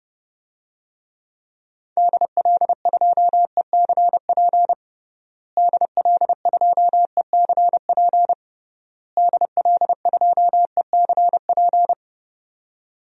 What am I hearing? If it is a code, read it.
Morse code, heard as a keyed tone, about 31 wpm, 700 Hz. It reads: DL2ECP DL2ECP DL2ECP
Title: DXCC-100 Call Signs - 6 Characters - Repeated 3x - 30wpm